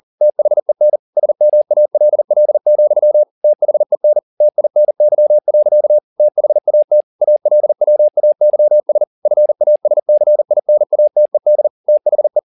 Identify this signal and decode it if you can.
Morse code, heard as a keyed tone, about 40 wpm, 600 Hz. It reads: THEN SMALL, THEN TINY. THAT ALWAYS FASCINATED THE
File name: the old man and the sea 2 40wpm